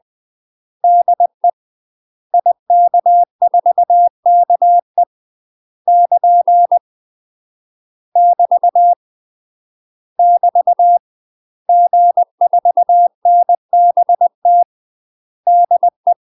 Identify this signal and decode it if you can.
Morse code, heard as a keyed tone, about 20 wpm, 700 Hz. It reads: DE IK4KE <KN> <BT> <BT> G4NBT DE
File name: UK MASTER QSOs TRAINING/20WPM